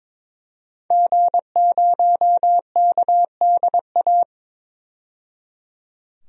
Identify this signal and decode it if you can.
Morse code, heard as a keyed tone, about 22 wpm, 700 Hz. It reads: G0KDA
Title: DXCC-100 Call Signs - 5 Characters - 22wpm